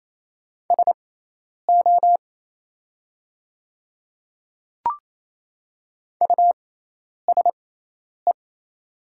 Morse code, heard as a keyed tone, about 28 wpm, spaced as for 10 wpm, 700 Hz. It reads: SO USE
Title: 6x Character Spacing - Top 200 Words - 28wpm